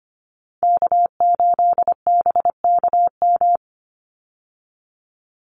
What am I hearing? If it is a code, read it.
Morse code, heard as a keyed tone, about 25 wpm, 700 Hz. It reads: K8BKM
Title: Call Signs Repeated 2x - 25wpm